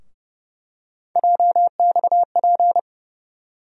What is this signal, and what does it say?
Morse code, heard as a keyed tone, about 30 wpm, 700 Hz. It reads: JXP
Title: USA Call Sign Postfixes - 3 Letters 30wpm